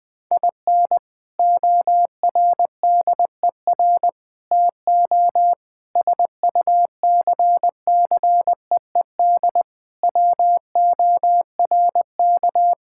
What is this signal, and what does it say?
Morse code, heard as a keyed tone, about 20 wpm, 700 Hz. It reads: IN ORDER TO SUCCEED WORK
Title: Sentences with 3-Word Phrases - Spoken Beginning - 20wpm